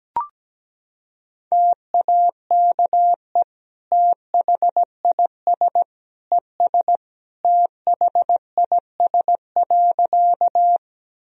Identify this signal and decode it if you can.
Morse code, heard as a keyed tone, about 17 wpm, 700 Hz. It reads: TAKE THIS ES THIS.
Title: Sentences from Top 200 Words and Abbreviations 17wpm